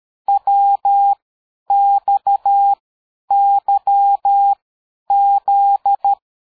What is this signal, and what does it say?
Morse code, heard as a keyed tone, about 13 wpm, 800 Hz. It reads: W X Y Z
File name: dk5ke / AbisZ 60BpM